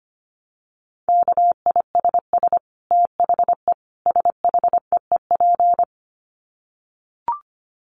Morse code, heard as a keyed tone, about 25 wpm, 700 Hz. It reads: KISS THE SHEEP
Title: Dit Challenge 25wpm